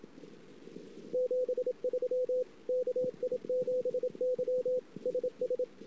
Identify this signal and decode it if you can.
Morse code, heard as a keyed tone, about 27 wpm, 500 Hz. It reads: 73 KI7Y SS
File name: ss-cw